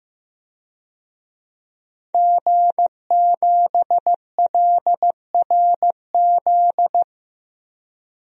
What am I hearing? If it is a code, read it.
Morse code, heard as a keyed tone, about 15 wpm, 700 Hz. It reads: G7LRZ
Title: DXCC-100 Call Signs - 5 Characters - Repeated 2x - 15wpm